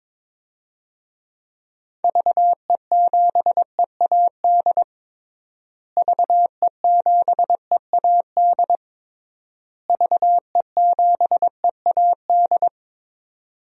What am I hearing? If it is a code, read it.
Morse code, heard as a keyed tone, about 22 wpm, 700 Hz. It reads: VE7EAD VE7EAD VE7EAD
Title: DXCC-100 Call Signs - 6 Characters - Repeated 3x - 22wpm